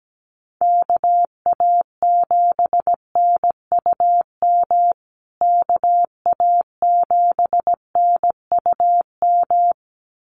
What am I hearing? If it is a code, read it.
Morse code, heard as a keyed tone, about 17 wpm, 700 Hz. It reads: KA7NUM KA7NUM
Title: USA Call Signs Repeated 2x 17wpm